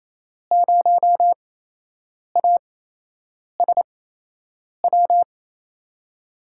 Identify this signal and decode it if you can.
Morse code, heard as a keyed tone, about 28 wpm, spaced as for 8 wpm, 700 Hz. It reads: 0ASW
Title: USA Call Signs - 8x Character Spacing - 28wpm